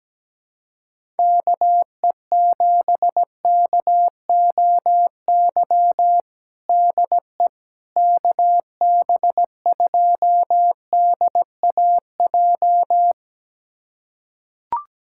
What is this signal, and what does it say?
Morse code, heard as a keyed tone, about 17 wpm, 700 Hz. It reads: KE7KOY DE KB2DAJ
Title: USA Callsign DE Callsign - 6 Characters - 17wpm